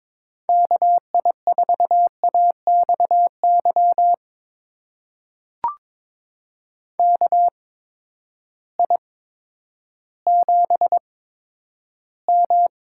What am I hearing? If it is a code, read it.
Morse code, heard as a keyed tone, about 22 wpm, spaced as for 11 wpm, 700 Hz. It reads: KI4AXY KI7M
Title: USA Call Signs - 8x Character Spacing - 22wpm